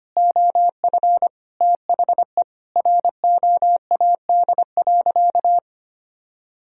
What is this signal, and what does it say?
Morse code, heard as a keyed tone, about 25 wpm, 700 Hz. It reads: OF THE ROAD.
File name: practice / 25wpm / Sentences with 4-Word Phrases - 25wpm